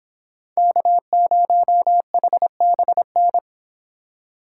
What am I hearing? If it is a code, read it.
Morse code, heard as a keyed tone, about 26 wpm, 700 Hz. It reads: K0HBN